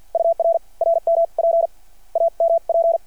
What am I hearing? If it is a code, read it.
Morse code, heard as a keyed tone, about 49 wpm, spaced as for 42 wpm, 660 Hz. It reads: UR RMP AMP
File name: archived / CRSnet